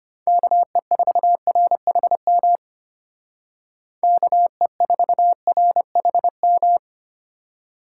Morse code, heard as a keyed tone, about 31 wpm, 700 Hz. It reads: KE4RHM KE4RHM
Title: Speed Racing - USA Call Signs 25wpm